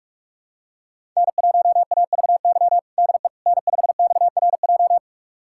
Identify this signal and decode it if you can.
Morse code, heard as a keyed tone, about 46 wpm, 700 Hz. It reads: N1AVY DE N5XRJ
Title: USA Callsign DE Callsign - 5 Characters - 45wpm